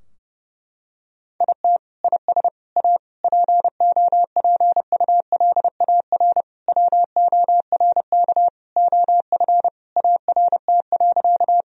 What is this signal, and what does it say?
Morse code, heard as a keyed tone, about 30 wpm, 700 Hz. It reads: IT IS A POPULAR WORK OF ART.
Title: Sentences from Top 1000 Words - Review Part 3 - 30wpm